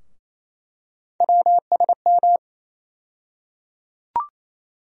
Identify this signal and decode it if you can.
Morse code, heard as a keyed tone, about 28 wpm, 700 Hz. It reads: WSM
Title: USA Call Sign Postfixes - 3 Letters 28wpm